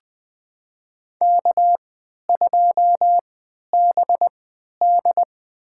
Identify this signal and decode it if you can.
Morse code, heard as a keyed tone, about 21 wpm, spaced as for 11 wpm, 700 Hz. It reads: K2BD
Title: USA Call Signs - 4 Characters - 3x Character Spacing - 20wpm